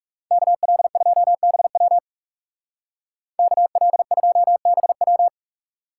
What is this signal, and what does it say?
Morse code, heard as a keyed tone, about 35 wpm, 700 Hz. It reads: KL2BW KL2BW
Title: Speed Racing - USA 2x2 Call Signs 30wpm